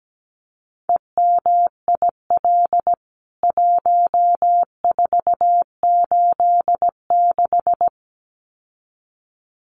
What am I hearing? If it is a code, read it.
Morse code, heard as a keyed tone, about 17 wpm, 700 Hz. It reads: EMIL 1486
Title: CWT Exchange 17wpm